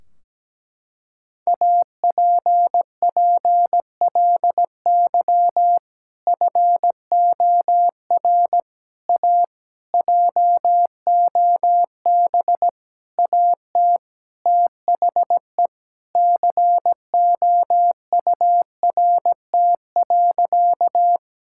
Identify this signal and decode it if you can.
Morse code, heard as a keyed tone, about 17 wpm, 700 Hz. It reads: APPLY FOR A JOB AT THE COURT.
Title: Sentences from Top 1600 Words 17wpm